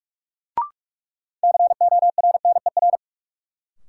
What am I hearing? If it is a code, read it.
Morse code, heard as a keyed tone, about 44 wpm, 700 Hz. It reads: CORNER
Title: Sentences from Top 1600 Words - New Word Drill - 45wpm